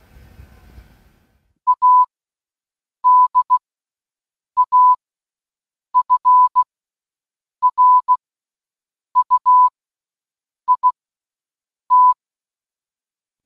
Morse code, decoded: ADAFRUIT